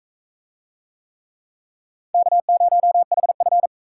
Morse code, heard as a keyed tone, about 42 wpm, 700 Hz. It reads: K0HF